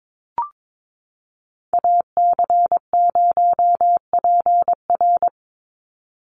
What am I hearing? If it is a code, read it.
Morse code, heard as a keyed tone, about 22 wpm, 700 Hz. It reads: AC0PR
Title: NA SOTA Activator Callsigns 22wpm